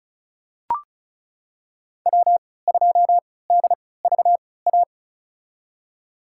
Morse code, heard as a keyed tone, about 36 wpm, spaced as for 20 wpm, 700 Hz. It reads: W2DVA
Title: USA Call Signs - 5 Characters - 3x Character Spacing - 35wpm